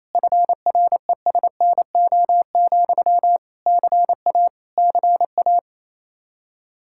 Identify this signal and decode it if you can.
Morse code, heard as a keyed tone, about 28 wpm, 700 Hz. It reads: FRESNO, CA CA
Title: Practice QSO - Exchange 1 - Beginning - 28wpm